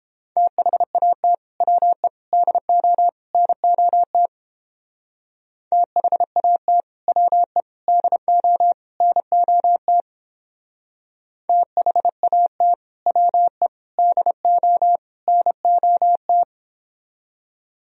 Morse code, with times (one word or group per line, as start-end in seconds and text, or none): 0.4-1.3 THAT
1.6-2.1 WE
2.3-3.1 DO
3.3-4.3 NOT
5.7-6.8 THAT
7.1-7.6 WE
7.9-8.7 DO
9.0-10.0 NOT
11.5-12.7 THAT
13.1-13.7 WE
14.0-15.0 DO
15.3-16.4 NOT